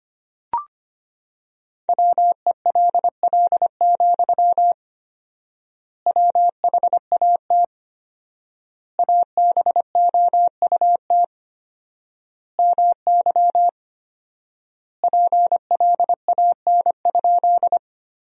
Morse code, heard as a keyed tone, about 25 wpm, 700 Hz. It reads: WELL, WHAT ABOUT MY PLAN?
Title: 4x Word-Spacing - Sentences from Top 500 Words - 25wpm